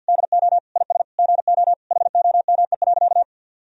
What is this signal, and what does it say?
Morse code, heard as a keyed tone, about 50 wpm, 700 Hz. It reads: DO IS GO HOME.
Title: Sentences from Top 200 Words and Abbreviations 50wpm